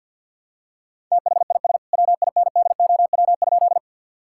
Morse code, heard as a keyed tone, about 46 wpm, 700 Hz. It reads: THIS WINDOW?
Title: Sentences from Top 1600 Words 50wpm